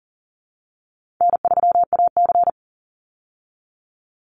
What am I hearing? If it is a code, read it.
Morse code, heard as a keyed tone, about 40 wpm, 700 Hz. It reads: N3AC
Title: Easy Call Signs Repeated 2x - 40wpm